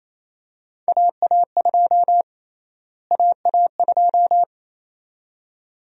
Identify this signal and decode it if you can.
Morse code, heard as a keyed tone, about 28 wpm, 700 Hz. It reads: AA2 AA2
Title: USA Call Sign Prefixes - Repeated 2x - 28wpm